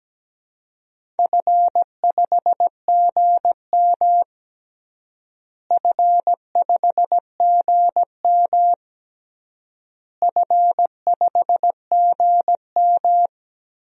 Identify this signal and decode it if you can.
Morse code, heard as a keyed tone, about 17 wpm, 700 Hz. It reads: F5GM F5GM F5GM
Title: DXCC-100 Call Signs - 4 Characters - Repeated 3x - 17wpm